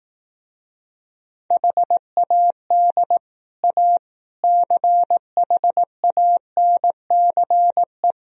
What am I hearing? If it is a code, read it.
Morse code, heard as a keyed tone, about 18 wpm, 700 Hz. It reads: HAD A CHANCE
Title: Speed-Racing - Sets of 4 Words 15wpm